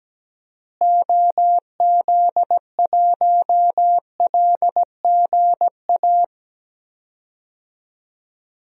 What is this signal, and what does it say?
Morse code, heard as a keyed tone, about 17 wpm, 700 Hz. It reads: OZ1LGA
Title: DXCC-100 Call Signs - 6 Characters - Repeated 2x - 17wpm